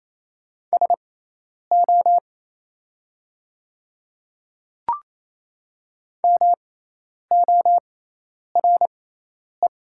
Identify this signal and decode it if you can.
Morse code, heard as a keyed tone, about 28 wpm, spaced as for 10 wpm, 700 Hz. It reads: SO MORE
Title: 6x Character Spacing - Top 100 Words - 28wpm